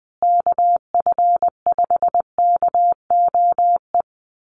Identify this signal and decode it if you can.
Morse code, heard as a keyed tone, about 20 wpm, 700 Hz. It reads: KF5KOE
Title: USA Call Signs Repeated 3x 20wpm